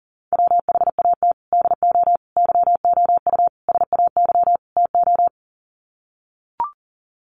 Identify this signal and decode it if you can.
Morse code, heard as a keyed tone, about 40 wpm, 700 Hz. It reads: WHAT DO YOU SAY TO
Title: Sets of 5 Words 40wpm